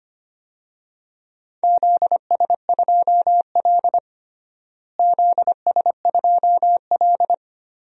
Morse code, heard as a keyed tone, about 25 wpm, 700 Hz. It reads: ZS2L ZS2L
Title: DXCC-100 Call Signs - 4 Characters - Repeated 3x - 25wpm